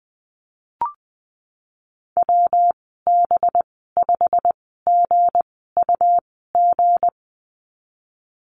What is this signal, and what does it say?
Morse code, heard as a keyed tone, about 20 wpm, spaced as for 15 wpm, 700 Hz. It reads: WB5GUG
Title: 219 Lesson 35g - Number 0 - Call Signs 20wpm